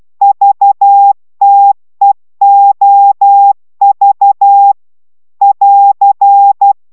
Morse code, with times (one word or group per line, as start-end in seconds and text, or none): 0.2-4.7 VTEOV
5.4-6.7 <AR>